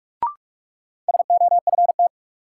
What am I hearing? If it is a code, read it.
Morse code, heard as a keyed tone, about 46 wpm, 700 Hz. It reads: SOFT
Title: Sentences from Top 1500 Words - New Word Drill Alt - 45wpm